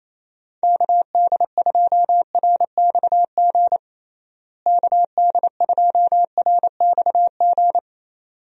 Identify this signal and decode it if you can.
Morse code, heard as a keyed tone, about 28 wpm, 700 Hz. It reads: KD2RXG KD2RXG
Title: USA Call Signs - 6 Characters - Repeated 2x - 28wpm